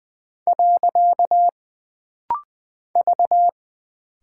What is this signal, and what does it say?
Morse code, heard as a keyed tone, about 20 wpm, 700 Hz. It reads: . V
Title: Single Character (Letter:Number:Punctuation:Prosign) - Rapid-Fire - 20wpm